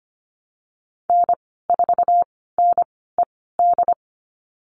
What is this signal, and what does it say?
Morse code, heard as a keyed tone, about 25 wpm, spaced as for 16 wpm, 700 Hz. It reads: N4NED
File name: Lesson 15g - Number 5 - Call Signs 25wpm